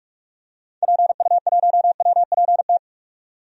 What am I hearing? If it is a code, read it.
Morse code, heard as a keyed tone, about 46 wpm, 700 Hz. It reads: PU1WPT
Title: DXCC-100 Call Signs - 6 Characters - 45wpm